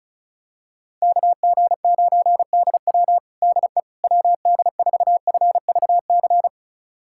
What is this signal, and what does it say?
Morse code, heard as a keyed tone, about 36 wpm, 700 Hz. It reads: KG9DW DE WD4FVC
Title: USA Callsign DE Callsign 35wpm